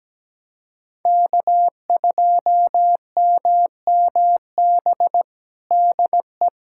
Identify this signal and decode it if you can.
Morse code, heard as a keyed tone, about 17 wpm, 700 Hz. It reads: K2MMB DE
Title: USA Callsign DE Callsign - 5 Characters - 17wpm